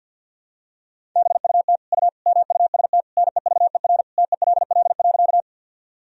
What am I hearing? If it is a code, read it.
Morse code, heard as a keyed tone, about 52 wpm, 700 Hz. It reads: BUT U MUST NEVER TELL.
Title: Sentences from Top 300 Words and Abbreviations 50wpm